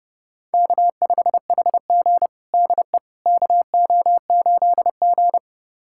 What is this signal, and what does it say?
Morse code, heard as a keyed tone, about 30 wpm, 700 Hz. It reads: K5HG DE KO8G
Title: USA Callsign DE Callsign - 4 Characters - 30wpm